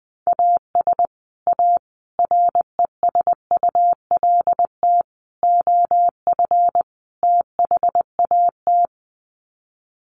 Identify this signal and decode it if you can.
Morse code, heard as a keyed tone, about 20 wpm, 700 Hz. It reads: AS A RESULT OF THAT
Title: Sets of 5 Words Encore 20wpm